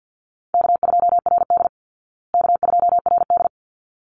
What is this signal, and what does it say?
Morse code, heard as a keyed tone, about 48 wpm, 700 Hz. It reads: K2RD K2RD
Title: Call Signs Repeated 3x - 50wpm